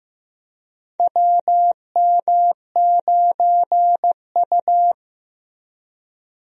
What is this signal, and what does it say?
Morse code, heard as a keyed tone, about 15 wpm, 700 Hz. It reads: WM9U
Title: USA Call Signs - 4 Characters - Repeated 2x - 15wpm